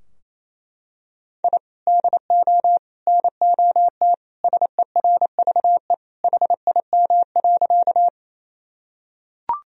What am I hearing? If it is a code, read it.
Morse code, heard as a keyed tone, about 28 wpm, 700 Hz. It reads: I DO NOT SERVE HIM.
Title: Sentences from Top 1000 Words - Review Part 2 - 28wpm